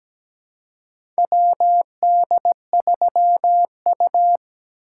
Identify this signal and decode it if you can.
Morse code, heard as a keyed tone, about 17 wpm, 700 Hz. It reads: WD3U
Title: USA Call Signs - 4 Characters - 2x Character Spacing - 17wpm